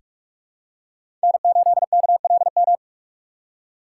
Morse code, heard as a keyed tone, about 46 wpm, 700 Hz. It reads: N8KLM
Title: USA Call Signs - 5 Characters - 6x Character Spacing - 45wpm